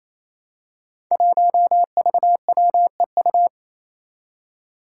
Morse code, heard as a keyed tone, about 27 wpm, 700 Hz. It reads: 1VWEU
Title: Five Challenging Letter-Number Combinations 28wpm